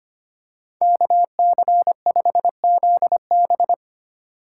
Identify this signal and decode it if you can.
Morse code, heard as a keyed tone, about 25 wpm, 700 Hz. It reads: KC5ZB
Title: USA Call Signs - 3x Character Spacing - 25wpm